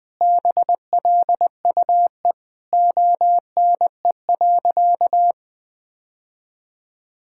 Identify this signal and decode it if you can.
Morse code, heard as a keyed tone, about 21 wpm, 700 Hz. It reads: BLUE ONE.
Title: Sentences with 2-Word Phrases - 20wpm